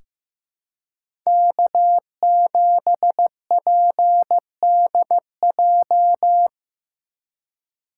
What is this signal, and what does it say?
Morse code, heard as a keyed tone, about 15 wpm, 700 Hz. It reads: K7PDJ
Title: USA Call Signs - 5 Characters - 3x Character Spacing - 15wpm